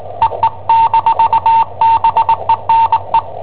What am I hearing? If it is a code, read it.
Morse code, heard as a keyed tone, about 20 wpm, 910 Hz. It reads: EE-BENE
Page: Cassetta esercizi preparata da I7OHP per IW5ELP